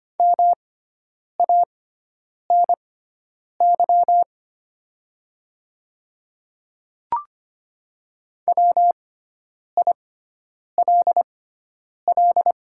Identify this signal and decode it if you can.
Morse code, heard as a keyed tone, about 25 wpm, spaced as for 9 wpm, 700 Hz. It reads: MANY WILL
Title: 6x Character Spacing - Top 100 Words - 25wpm